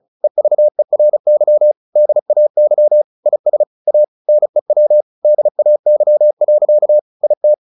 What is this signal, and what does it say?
Morse code, heard as a keyed tone, about 34 wpm, 600 Hz. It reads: EVERY DAY IS A NEW DAY. IT
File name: the old man and the sea / the old man and the sea 1 35wpm